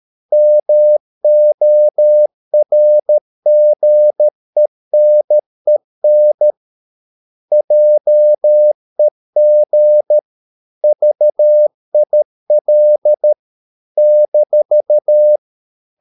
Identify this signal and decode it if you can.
Morse code, heard as a keyed tone, about 13 wpm, 600 Hz. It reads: MORGENEN JEG VIL -
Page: Ferie 13 wpm